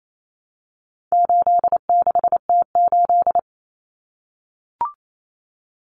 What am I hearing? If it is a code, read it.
Morse code, heard as a keyed tone, about 28 wpm, 700 Hz. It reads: 86T8
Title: Common Cut Numbers - 4-Digit 28wpm